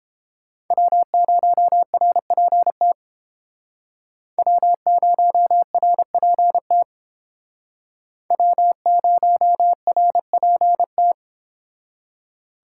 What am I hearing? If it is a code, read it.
Morse code, heard as a keyed tone, about 29 wpm, 700 Hz. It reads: W0RPT W0RPT W0RPT